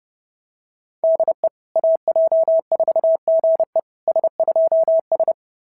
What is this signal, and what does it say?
Morse code, heard as a keyed tone, about 30 wpm, 690 Hz. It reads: DE AJ4GE S2S
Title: SOTA QSO Practice 30wpm